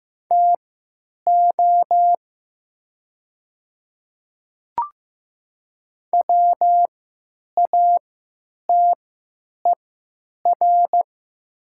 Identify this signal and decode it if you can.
Morse code, heard as a keyed tone, about 15 wpm, spaced as for 9 wpm, 700 Hz. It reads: TO WATER
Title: 3x Character Spacing - Top 100 Words - 15wpm